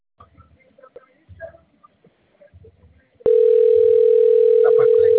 Ringback tone (North America), 440 + 480 Hz.
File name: rg-1600-unknown- - - .57459